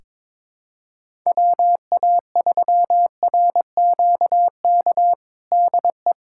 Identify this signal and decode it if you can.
Morse code, heard as a keyed tone, about 22 wpm, 700 Hz. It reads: WA3RQK DE
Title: USA Callsign DE Callsign - 6 Characters - 22wpm